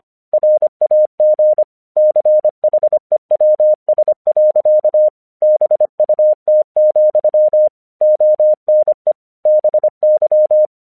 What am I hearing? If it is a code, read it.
Morse code, heard as a keyed tone, about 26 wpm, 610 Hz. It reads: RAG CHEWS. BUT, ONE BY
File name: 1 English 25wpm / AQC13